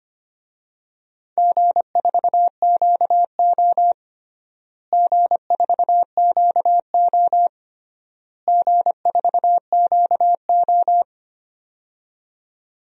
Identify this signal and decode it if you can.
Morse code, heard as a keyed tone, about 25 wpm, 700 Hz. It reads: G4QO G4QO G4QO
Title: DXCC-100 Call Signs - 4 Characters - Repeated 3x - 25wpm